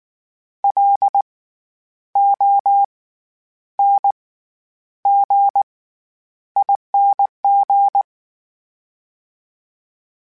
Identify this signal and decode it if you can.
Morse code, heard as a keyed tone, about 19 wpm, spaced as for 9 wpm, 800 Hz. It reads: LONGING
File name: Special INGs / Special INGs-mp3s / SlowToFaster ing @18